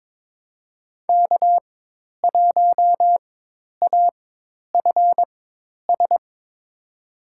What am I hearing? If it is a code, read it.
Morse code, heard as a keyed tone, about 22 wpm, spaced as for 10 wpm, 700 Hz. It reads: K1AFS
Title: USA Call Signs - 5 Characters - 4x Character Spacing - 22wpm